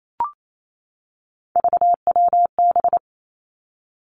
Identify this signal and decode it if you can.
Morse code, heard as a keyed tone, about 28 wpm, 700 Hz. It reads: VWB
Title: Three Challenging Letter-Number Combinations 28wpm